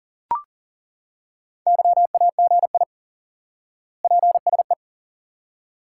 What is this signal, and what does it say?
Morse code, heard as a keyed tone, about 40 wpm, 700 Hz. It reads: YAGI PSE
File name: IQE Course 3 - Lesson 89 - YAGI - Two Elements - 40wpm